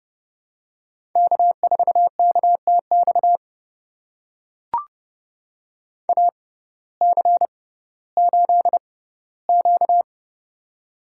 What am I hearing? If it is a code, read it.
Morse code, heard as a keyed tone, about 30 wpm, spaced as for 17 wpm, 700 Hz. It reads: K4KTX AC8Q
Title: USA Call Signs - 6x Character Spacing - 30wpm